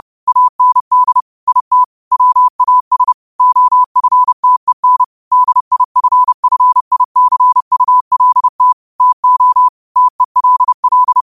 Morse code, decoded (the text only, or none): AND IT WAS OFTEN DIFFICULT TO TELL